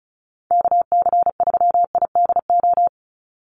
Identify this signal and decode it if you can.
Morse code, heard as a keyed tone, about 34 wpm, 700 Hz. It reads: KC3IDO
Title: USA Call Signs 35wpm